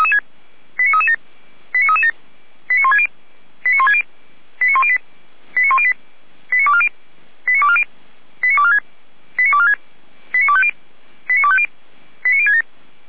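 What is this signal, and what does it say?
Five-tone selective call (ZVEI), heard as quick runs of five tones, tones of about 70 ms. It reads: 89337 89337 89160 89160 89108 89108 89240 89240 89246 89246 89250 89250 89067
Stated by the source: ZVEI five-tone selective call received by SDR.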